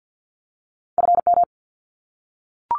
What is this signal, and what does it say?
Morse code, heard as a keyed tone, about 50 wpm, 700 Hz. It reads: FM
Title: Abbreviations 50wpm